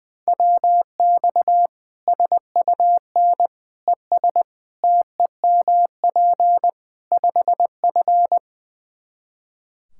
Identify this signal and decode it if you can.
Morse code, heard as a keyed tone, about 19 wpm, 700 Hz. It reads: WX SUN ES TEMP 5F
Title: Practice QSO - Exchange 2 - Info - 20wpm